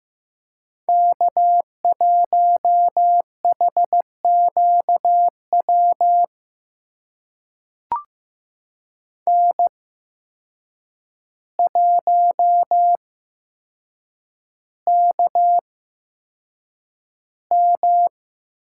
Morse code, heard as a keyed tone, about 15 wpm, spaced as for 7 wpm, 700 Hz. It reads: K1HQW N1KM
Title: USA Call Signs - 5 Characters - 8x Character Spacing - 15wpm